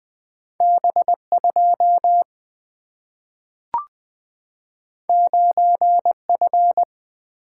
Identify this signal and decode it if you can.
Morse code, heard as a keyed tone, about 20 wpm, 700 Hz. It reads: B 2 9 F
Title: Two Letter-Number Combinations - 20wpm